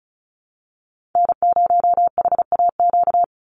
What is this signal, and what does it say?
Morse code, heard as a keyed tone, about 34 wpm, 700 Hz. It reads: N0HAQ
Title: USA Call Signs 35wpm